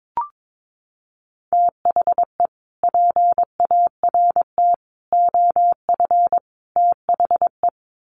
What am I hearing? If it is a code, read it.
Morse code, heard as a keyed tone, about 22 wpm, 700 Hz. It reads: THE PART OF THE
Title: Sets of 4 Words 22wpm